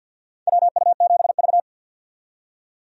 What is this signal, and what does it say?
Morse code, heard as a keyed tone, about 47 wpm, 700 Hz. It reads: WU7V
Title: USA Call Signs - 4 Characters - 50wpm